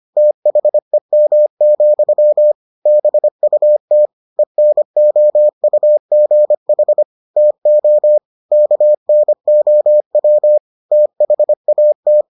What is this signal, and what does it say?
Morse code, heard as a keyed tone, about 25 wpm, 600 Hz. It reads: THEM, BUT ENOUGH TO KNOW THAT